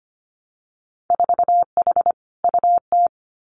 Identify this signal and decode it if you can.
Morse code, heard as a keyed tone, about 25 wpm, 690 Hz. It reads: 4H UT
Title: Winter Field Day Exchanges Encore 25wpm